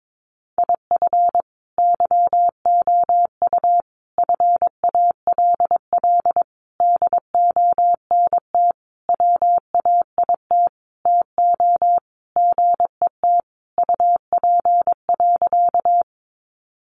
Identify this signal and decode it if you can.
Morse code, heard as a keyed tone, about 22 wpm, 700 Hz. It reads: IF YOU FALL DONT WAIT TO GET UP.
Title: Sentences from Top 500 Words 22wpm